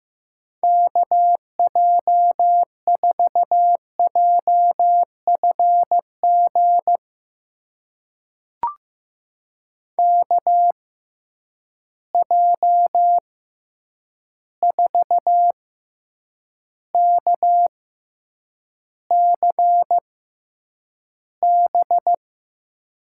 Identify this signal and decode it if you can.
Morse code, heard as a keyed tone, about 15 wpm, spaced as for 8 wpm, 700 Hz. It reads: KJ4JFG KJ4KCB
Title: USA Call Signs - 6 Characters - 6x Character Spacing - 15wpm